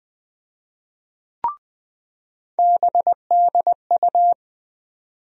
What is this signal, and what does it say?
Morse code, heard as a keyed tone, about 20 wpm, 700 Hz. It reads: BDU
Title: USA Call Sign Postfixes - 3 Letters 20wpm